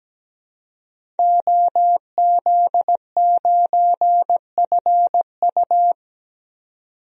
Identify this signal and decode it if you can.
Morse code, heard as a keyed tone, about 17 wpm, 700 Hz. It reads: OZ9FU
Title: DXCC-100 Call Signs - 5 Characters - 17wpm